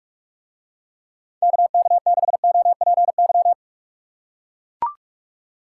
Morse code, heard as a keyed tone, about 44 wpm, 700 Hz. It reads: KK6OPY